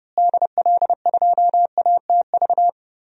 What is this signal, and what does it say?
Morse code, heard as a keyed tone, about 30 wpm, 700 Hz. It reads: DL2ATV
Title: DXCC-100 Call Signs - 6 Characters - Repeated 3x - 30wpm